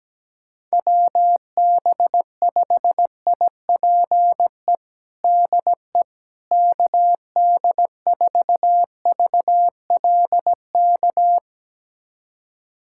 Morse code, heard as a keyed tone, about 17 wpm, 700 Hz. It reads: WB5IPE DE KD4VLK
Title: USA Callsign DE Callsign 17wpm